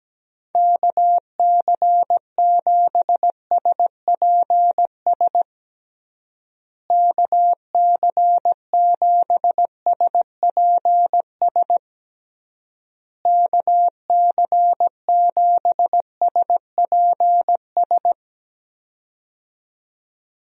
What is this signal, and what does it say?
Morse code, heard as a keyed tone, about 17 wpm, 700 Hz. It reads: KC7SPS KC7SPS KC7SPS
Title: USA Call Signs - 6 Characters - Repeated 3x - 17wpm